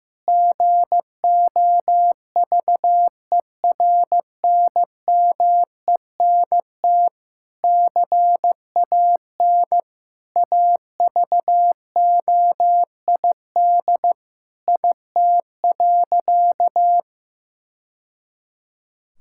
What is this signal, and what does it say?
Morse code, heard as a keyed tone, about 15 wpm, 700 Hz. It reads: GOVERNMENT CAN AVOID IT.
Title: Sentences from Top 1000 Words - Review Part 6 - 15wpm